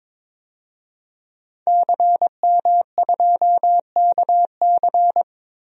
Morse code, heard as a keyed tone, about 22 wpm, 700 Hz. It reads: CM2KC
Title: DXCC-100 Call Signs - 5 Characters - 22wpm